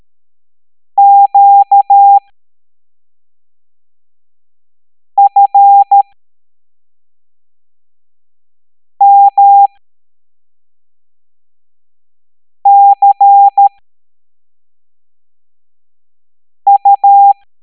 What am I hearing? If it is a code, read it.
Morse code, heard as a keyed tone, about 13 wpm, spaced as for 3 wpm, 800 Hz. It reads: QFMCU
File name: hamdata / L07-EX03-HQP